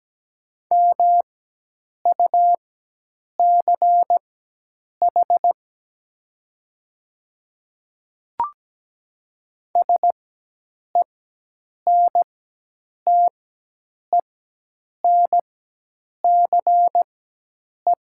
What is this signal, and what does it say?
Morse code, heard as a keyed tone, about 17 wpm, spaced as for 8 wpm, 700 Hz. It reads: MUCH SENTENCE
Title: 4x Character Spacing - Top 200 Words - 17wpm